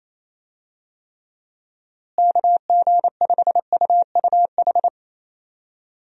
Morse code, heard as a keyed tone, about 28 wpm, 700 Hz. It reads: KG5UUH
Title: USA Call Signs - 6 Characters - Repeated 3x - 28wpm